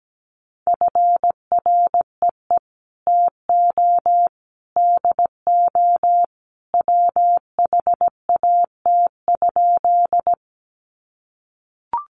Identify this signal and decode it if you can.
Morse code, heard as a keyed tone, about 17 wpm, 700 Hz. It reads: FREE TO DO WHAT?
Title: Sentences from Top 500 Words 17wpm